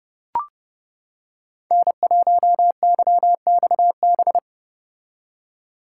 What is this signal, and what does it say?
Morse code, heard as a keyed tone, about 32 wpm, 700 Hz. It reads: N1YXB